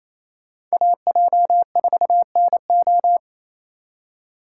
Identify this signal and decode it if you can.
Morse code, heard as a keyed tone, about 36 wpm, 700 Hz. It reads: AJ4NO